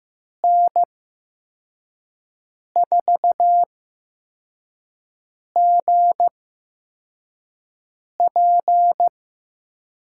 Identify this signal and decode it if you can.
Morse code, heard as a keyed tone, about 15 wpm, spaced as for 4 wpm, 700 Hz. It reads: N4GP
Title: USA Call Signs - 8x Character Spacing - 15wpm